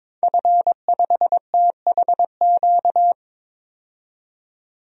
Morse code, heard as a keyed tone, about 21 wpm, 700 Hz. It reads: F5THQ
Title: DXCC-100 Call Signs - 5 Characters - 22wpm